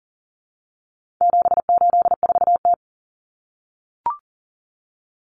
Morse code, heard as a keyed tone, about 40 wpm, 700 Hz. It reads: 784T
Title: Common Cut Numbers - 4-Digit 40wpm